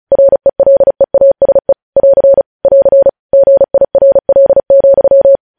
Morse code, heard as a keyed tone, about 36 wpm, 550 Hz. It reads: RELEASE <AR> <AR> GIRL,
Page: news headlines in morse code at 35 wpm